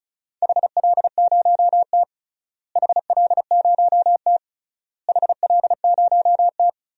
Morse code, heard as a keyed tone, about 34 wpm, 700 Hz. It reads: HL0T HL0T HL0T
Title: DXCC-100 Call Signs - 4 Characters - Repeated 3x - 35wpm